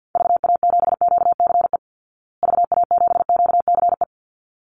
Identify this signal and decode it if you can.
Morse code, heard as a keyed tone, about 50 wpm, 700 Hz. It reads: VA7QCE VA7QCE
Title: Call Signs Repeated 3x - 50wpm